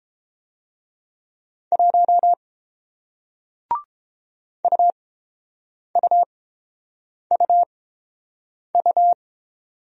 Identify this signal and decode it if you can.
Morse code, heard as a keyed tone, about 30 wpm, spaced as for 7 wpm, 700 Hz. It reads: 1 UUUU